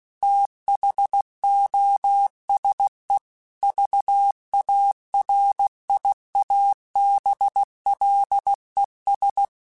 Morse code, heard as a keyed tone, about 16 wpm, 800 Hz. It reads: THOSE VARIABLES
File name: Technical Subjects / Technical Subjects-mp3s / AND Gate @15